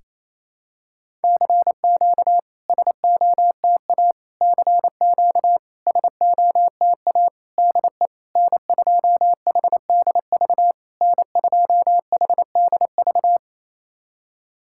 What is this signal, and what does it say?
Morse code, heard as a keyed tone, about 28 wpm, 700 Hz. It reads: CQ SOTA CQ SOTA DE N2HDV N2HDV
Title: SOTA QSO Practice 28wpm